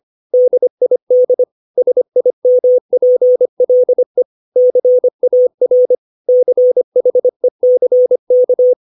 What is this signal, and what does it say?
Morse code, heard as a keyed tone, about 25 wpm, 500 Hz. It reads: DID SIMPLE CAR CHECK
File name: 632 words / 632 Words 25WPM